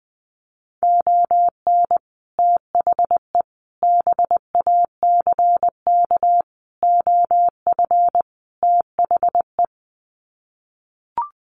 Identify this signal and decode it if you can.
Morse code, heard as a keyed tone, about 20 wpm, 700 Hz. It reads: ON THE BACK OF THE
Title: Sets of 5 Words 20wpm